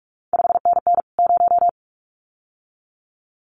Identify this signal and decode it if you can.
Morse code, heard as a keyed tone, about 46 wpm, 700 Hz. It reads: 5NN 0
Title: CQ WW DX Contest Exchange Encore - 45wpm